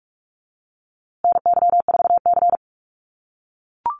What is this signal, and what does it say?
Morse code, heard as a keyed tone, about 44 wpm, 700 Hz. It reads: NY4C
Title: USA Call Signs Repeated 3x 45wpm